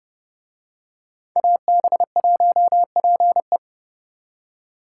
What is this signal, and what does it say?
Morse code, heard as a keyed tone, about 28 wpm, 700 Hz. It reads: AB1PE